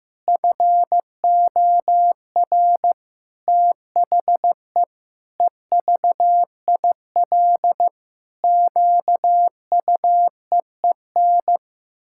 Morse code, heard as a keyed tone, about 15 wpm, 700 Hz. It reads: FOR THE EVIL QUEEN
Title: Pangram Challenge v2 - 15wpm